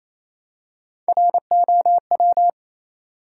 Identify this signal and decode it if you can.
Morse code, heard as a keyed tone, about 28 wpm, 700 Hz. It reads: ROW
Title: Three Letter Words - 28wpm